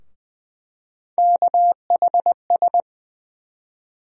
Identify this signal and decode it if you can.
Morse code, heard as a keyed tone, about 20 wpm, 700 Hz. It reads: KHS
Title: USA Call Sign Postfixes - 3 Letters 20wpm